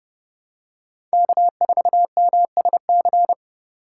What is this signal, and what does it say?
Morse code, heard as a keyed tone, about 30 wpm, 690 Hz. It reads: K4MSC
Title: USA Call Signs - 5 Characters - 2x Character Spacing - 30wpm